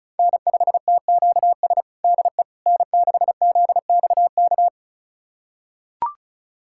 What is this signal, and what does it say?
Morse code, heard as a keyed tone, about 36 wpm, 700 Hz. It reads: N5TQS DE N6ZXK
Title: USA Callsign DE Callsign - 5 Characters - 35wpm